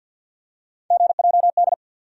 Morse code, heard as a keyed tone, about 49 wpm, 690 Hz. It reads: GJD
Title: USA Call Sign Postfixes - 3 Letters 50wpm